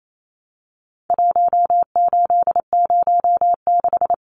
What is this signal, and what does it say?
Morse code, heard as a keyed tone, about 28 wpm, 700 Hz. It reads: 1806
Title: Numbers - 4-Digit 28wpm